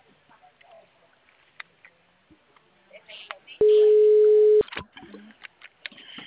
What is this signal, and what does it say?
Ringback tone (Europe, 425 Hz), with speech.
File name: out- -2006- - - .89106